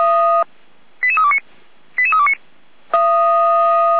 Five-tone selective call (ZVEI), heard as quick runs of five tones, tones of about 70 ms. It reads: 88329 88329